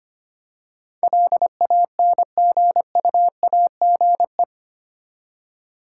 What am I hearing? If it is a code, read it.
Morse code, heard as a keyed tone, about 25 wpm, 700 Hz. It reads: LANGUAGE
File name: Eight Letter Words - 25wpm